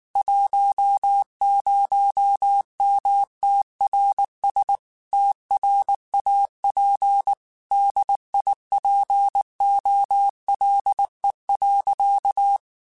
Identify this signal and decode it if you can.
Morse code, heard as a keyed tone, about 19 wpm, 800 Hz. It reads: 10MTRS TRAP DIPOLE.
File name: QSOs / QSOs-mp3s / WD0FBT 1@18